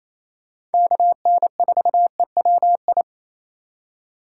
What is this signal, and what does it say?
Morse code, heard as a keyed tone, about 28 wpm, 700 Hz. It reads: KN4EWI
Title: K1USN SST QSO Practice - 28wpm